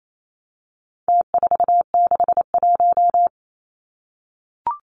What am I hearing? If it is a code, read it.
Morse code, heard as a keyed tone, about 28 wpm, 700 Hz. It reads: T461
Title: Numbers - Contest Style - 4-Digit 28wpm